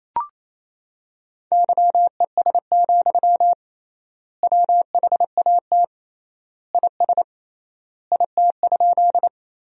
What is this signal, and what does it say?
Morse code, heard as a keyed tone, about 28 wpm, 700 Hz. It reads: YES, WHAT IS IT?
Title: 3x Word-Spacing - Sentences from Top 500 Words - 28wpm